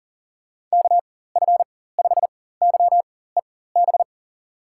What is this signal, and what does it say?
Morse code, heard as a keyed tone, about 39 wpm, spaced as for 18 wpm, 700 Hz. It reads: KF5YEB
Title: USA Call Signs - 6 Characters - 4x Character Spacing - 40wpm